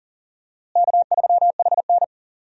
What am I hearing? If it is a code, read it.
Morse code, heard as a keyed tone, about 40 wpm, 700 Hz. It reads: K3HN
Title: USA Call Signs - 4 Characters - Repeated 3x - 40wpm